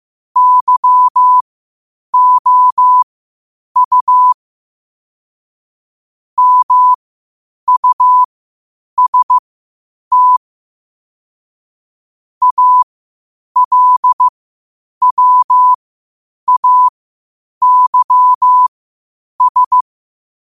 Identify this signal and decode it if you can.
Morse code, heard as a keyed tone, about 15 wpm, spaced as for 8 wpm, 1000 Hz. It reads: YOU MUST ALWAYS